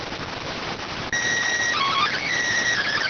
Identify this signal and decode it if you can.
SSTV (slow-scan television), PD 290.